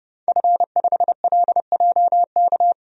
Morse code, heard as a keyed tone, about 31 wpm, 700 Hz. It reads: F5LJK
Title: DXCC-100 Call Signs - 5 Characters - 30wpm